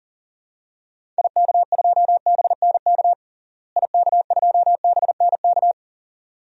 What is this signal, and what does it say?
Morse code, heard as a keyed tone, about 40 wpm, 700 Hz. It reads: IK2BNK IK2BNK
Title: DXCC-100 Call Signs - 6 Characters - Repeated 2x - 40wpm